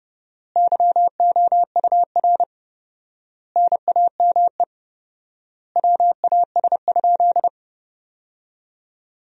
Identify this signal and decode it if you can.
Morse code, heard as a keyed tone, about 30 wpm, 700 Hz. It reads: YOUR NAME WAS?
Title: 4x Word-Spacing - Sentences from Top 200 Words - 30wpm